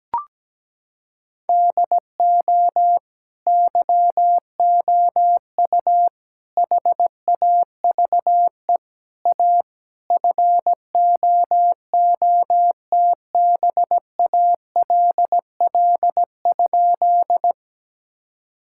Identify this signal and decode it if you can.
Morse code, heard as a keyed tone, about 17 wpm, 700 Hz. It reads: DO YOU HAVE A FOOTBALL?
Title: Sentences from Top 1600 Words 17wpm